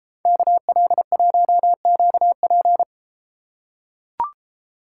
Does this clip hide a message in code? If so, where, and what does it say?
Yes: Morse code from 0.3 to 2.8 s, heard as a keyed tone, about 30 wpm, 700 Hz. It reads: KL1QP